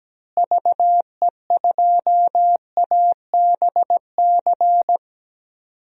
Morse code, heard as a keyed tone, about 17 wpm, 700 Hz. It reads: VE2ABC
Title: DXCC-100 Call Signs - 6 Characters - Repeated 3x - 17wpm